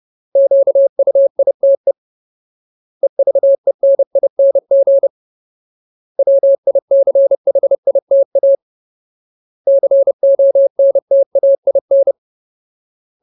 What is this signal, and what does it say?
Morse code, heard as a keyed tone, about 30 wpm, 550 Hz. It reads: QUITE EVENING WICHITA CONTAIN
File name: Ultimate Word List / 30WPM Ultimate